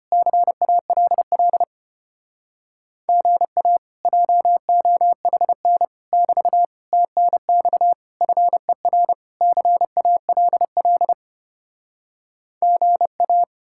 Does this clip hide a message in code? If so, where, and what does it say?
Yes: Morse code from 0.1 to 13.4 s, heard as a keyed tone, about 33 wpm, 700 Hz. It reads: CALL GA JOHN <BT> TNX FER CALL GA